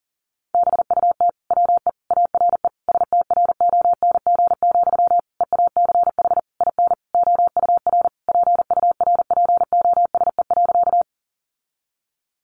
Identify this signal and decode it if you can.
Morse code, heard as a keyed tone, about 40 wpm, 700 Hz. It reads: BUT WE ARE STRONG, EACH IN OUR PURPOSE.
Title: Sentences from Top 800 Words 40wpm